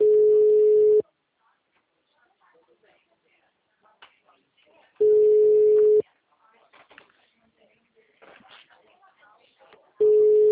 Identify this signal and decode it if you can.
Ringback tone (Europe, 425 Hz), with speech.